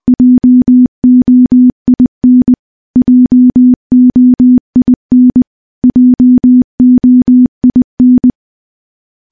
Morse code, heard as a keyed tone, about 20 wpm, 250 Hz. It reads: JOIN JOIN JOIN
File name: morse-training-20WPM